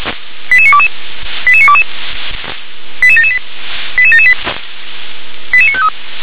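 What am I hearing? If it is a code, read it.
Five-tone selective call (ZVEI), heard as quick runs of five tones, tones of about 70 ms. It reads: 88922 88922 88778 88778 88953
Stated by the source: Five-tone ZVEI selective call 88740 received on an SDR.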